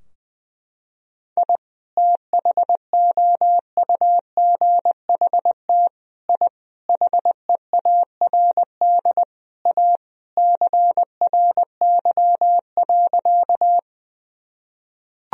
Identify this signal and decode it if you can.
Morse code, heard as a keyed tone, about 20 wpm, 700 Hz. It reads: I THOUGHT I HEARD A CRY.
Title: Sentences from Top 1000 Words - Review Part 2 - 20wpm